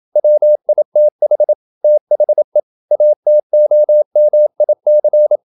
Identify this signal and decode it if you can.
Morse code, heard as a keyed tone, about 27 wpm, 600 Hz. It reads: WITH THE ATOMIC
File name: Never 0027wpm